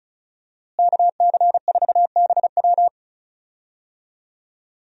Morse code, heard as a keyed tone, about 36 wpm, 700 Hz. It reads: KC4BW
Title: USA Call Signs - 5 Characters - Repeated 3x - 35wpm